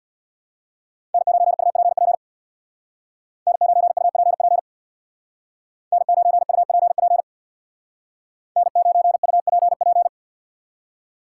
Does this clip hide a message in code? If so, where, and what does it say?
Yes: Morse code from 1.1 to 10.1 s, heard as a keyed tone, about 63 wpm, 700 Hz. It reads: N9UPP N9UPP N9UPP N9UPP